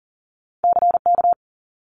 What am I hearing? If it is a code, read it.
Morse code, heard as a keyed tone, about 38 wpm, 700 Hz. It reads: CK
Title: Multi-letter Phonemes 40wpm